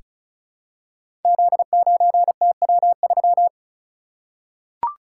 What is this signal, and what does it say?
Morse code, heard as a keyed tone, about 34 wpm, 700 Hz. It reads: Z9TW3
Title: Code Groups - 5 Letter:Numbers - 35wpm